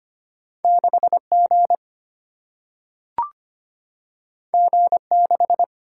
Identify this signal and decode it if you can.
Morse code, heard as a keyed tone, about 25 wpm, 700 Hz. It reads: 6 G G 6
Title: Single-Letter Single-Number 25wpm